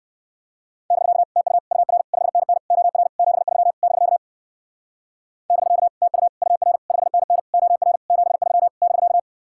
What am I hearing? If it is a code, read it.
Morse code, heard as a keyed tone, about 57 wpm, 700 Hz. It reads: <BK> TU UR 5NN OR 73 <BK> <BK> TU UR 5NN OR 73 <BK>